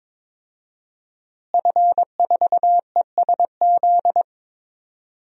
Morse code, heard as a keyed tone, about 22 wpm, 700 Hz. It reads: F4ESZ
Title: DXCC-100 Call Signs - 5 Characters - Repeated 2x - 22wpm